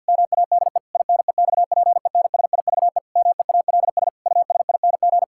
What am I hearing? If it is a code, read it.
Morse code, heard as a keyed tone, about 50 wpm, 700 Hz. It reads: MADE INEXPENSIVE MEALS USING
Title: Pangram Challenge v2 - 50wpm